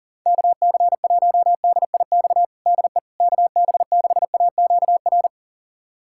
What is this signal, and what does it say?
Morse code, heard as a keyed tone, about 37 wpm, 700 Hz. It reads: KC1DIX DE KB6AQR
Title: USA Callsign DE Callsign 40wpm